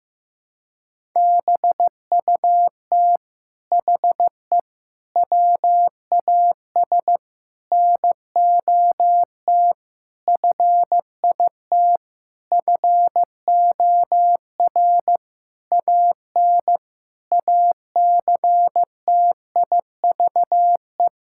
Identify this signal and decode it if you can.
Morse code, heard as a keyed tone, about 16 wpm, 700 Hz. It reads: BUT HE WAS NOT FIT FOR AN ACTIVE
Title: Sentences from Top 1000 Words - Review Part 6 - 15wpm